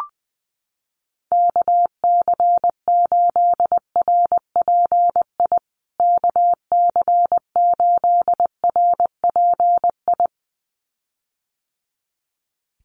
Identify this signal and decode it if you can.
Morse code, heard as a keyed tone, about 20 wpm, 700 Hz. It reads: KC8RPI KC8RPI
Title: USA Call Signs Repeated 2x 20wpm